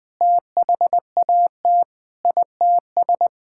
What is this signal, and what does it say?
Morse code, heard as a keyed tone, about 21 wpm, 700 Hz. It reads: THAT ITS
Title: Sentences with 6-Word Phrases - Spoken Beginning - 20wpm